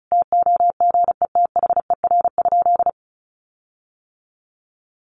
Morse code, heard as a keyed tone, about 34 wpm, 700 Hz. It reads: TOGETHER?
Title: Sentences from Top 1400 Words 35wpm